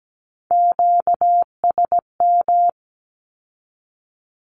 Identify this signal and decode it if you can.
Morse code, heard as a keyed tone, about 17 wpm, 700 Hz. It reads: QSM
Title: Q-Codes with Spoken Meaning - 17wpm